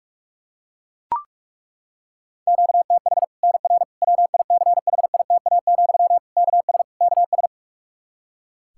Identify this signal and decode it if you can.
Morse code, heard as a keyed tone, about 41 wpm, 700 Hz. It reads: QTH NR WICHITA, KS KS
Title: Practice QSO - Exchange 1 - Beginning - 45wpm